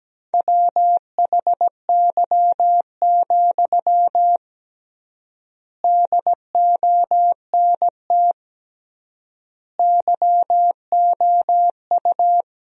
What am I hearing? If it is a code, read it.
Morse code, heard as a keyed tone, about 17 wpm, 700 Hz. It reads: WHY, DONT YOU
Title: 3x Word-Spacing - Sentences from Top 300 Words - 17wpm